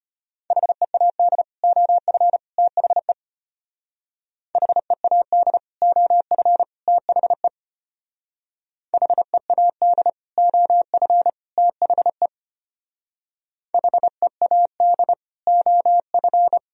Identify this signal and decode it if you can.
Morse code, heard as a keyed tone, about 32 wpm, 700 Hz. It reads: HEAD OF THE HEAD OF THE HEAD OF THE HEAD OF